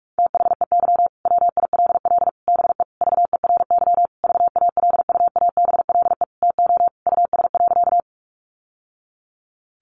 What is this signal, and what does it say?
Morse code, heard as a keyed tone, about 44 wpm, 700 Hz. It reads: THEY WILL BE VERY VALUABLE TO US.
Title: Sentences from Top 1100 Words 45wpm